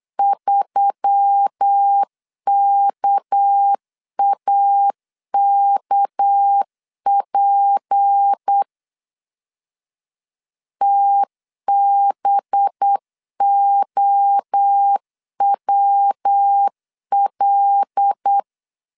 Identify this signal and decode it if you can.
Morse code, heard as a keyed tone, about 8 wpm, 800 Hz. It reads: 3KAKP TBOWL
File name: LEZIONI / 6 PKX0 / 6 PK 8wpm 3003